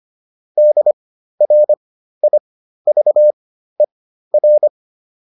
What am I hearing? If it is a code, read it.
Morse code, heard as a keyed tone, about 25 wpm, spaced as for 13 wpm, 600 Hz. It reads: DRIVER